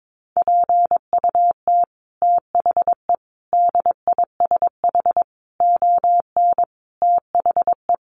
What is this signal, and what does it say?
Morse code, heard as a keyed tone, about 22 wpm, 700 Hz. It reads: PUT THE DISH ON THE
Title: Sentences from Top 1300 Words 22wpm